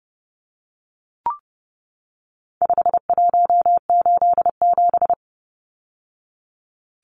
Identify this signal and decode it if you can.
Morse code, heard as a keyed tone, about 30 wpm, 700 Hz. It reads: 5187
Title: Numbers - 4-Digit 30wpm